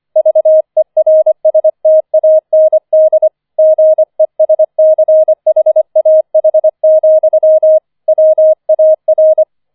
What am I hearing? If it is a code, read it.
Morse code, heard as a keyed tone, about 25 wpm, 630 Hz. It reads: VERSTAND GESCHAH, WAR